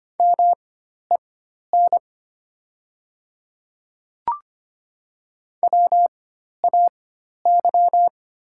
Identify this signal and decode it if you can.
Morse code, heard as a keyed tone, about 25 wpm, spaced as for 12 wpm, 690 Hz. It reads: MEN WAY
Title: 4x Character Spacing - Top 300 Words - 25wpm